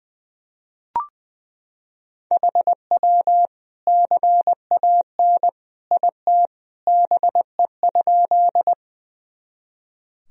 Morse code, heard as a keyed tone, about 20 wpm, 700 Hz. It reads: HW CAN IT BE?
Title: Sentences from Top 100 Words and Abbreviations 20wpm